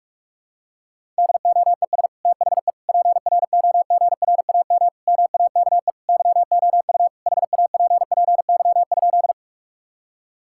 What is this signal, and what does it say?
Morse code, heard as a keyed tone, about 46 wpm, 700 Hz. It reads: DOES THE PROGRAM MAKE YOU HAPPY?
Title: Sentences from Top 1000 Words - Review Part 5 - 45wpm